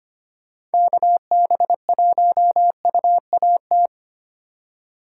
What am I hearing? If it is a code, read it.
Morse code, heard as a keyed tone, about 31 wpm, 700 Hz. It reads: KB1UAT
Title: Speed Racing - USA 2x3 Call Signs 25wpm